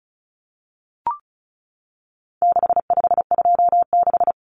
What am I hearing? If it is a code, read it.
Morse code, heard as a keyed tone, about 34 wpm, 700 Hz. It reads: 6526
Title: Numbers - 4-Digit 35wpm